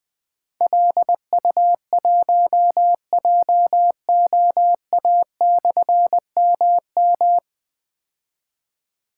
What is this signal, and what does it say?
Morse code, heard as a keyed tone, about 20 wpm, 700 Hz. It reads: LU1JOA/MM
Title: Difficult Call Signs 20wpm